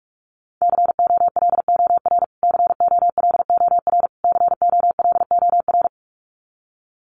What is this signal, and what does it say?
Morse code, heard as a keyed tone, about 44 wpm, 700 Hz. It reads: COLOR COLOR COLOR
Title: Top 300 Words Repeated 45wpm